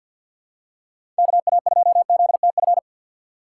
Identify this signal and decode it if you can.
Morse code, heard as a keyed tone, about 50 wpm, 700 Hz. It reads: KA2ZTF
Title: USA Call Signs - 6 Characters - 3x Character Spacing - 50wpm